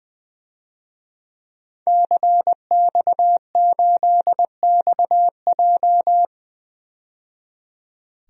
Morse code, heard as a keyed tone, about 20 wpm, 700 Hz. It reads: CX8XJ
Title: DXCC-100 Call Signs - 5 Characters - 20wpm